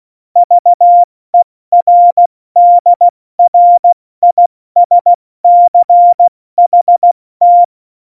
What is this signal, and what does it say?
Morse code, heard as a keyed tone, about 16 wpm, 700 Hz. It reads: VERDRISCHT
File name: Morsekurs / Witze / Witz 28 April 2020, 70 F80